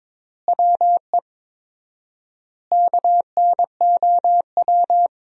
Morse code, heard as a keyed tone, about 22 wpm, 700 Hz. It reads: WE KNOW
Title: 4x Word-Spacing - Sentences from Top 500 Words - 22wpm